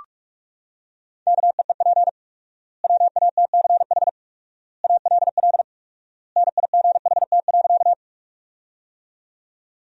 Morse code, heard as a keyed tone, about 46 wpm, 700 Hz. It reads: KEEP WATCH ALL NIGHT.
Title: 4x Word-Spacing - Sentences from Top 300 Words - 45wpm